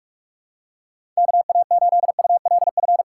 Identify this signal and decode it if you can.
Morse code, heard as a keyed tone, about 42 wpm, 700 Hz. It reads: KA8ULF
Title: USA Call Signs - 6 Characters - 4x Character Spacing - 45wpm